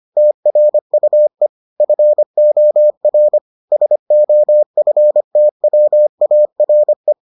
Morse code, heard as a keyed tone, about 25 wpm, 600 Hz. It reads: TRUE FOR SOFTWARE